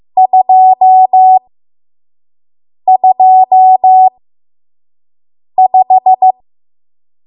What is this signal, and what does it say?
Morse code, heard as a keyed tone, about 15 wpm, spaced as for 5 wpm, 750 Hz. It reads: 225
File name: Morse / 05WPM